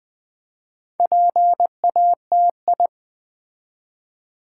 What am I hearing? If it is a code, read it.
Morse code, heard as a keyed tone, about 20 wpm, 700 Hz. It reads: PATI
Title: Build A Word - 4 Letters At A Time - 8-Letter Words - 20wpm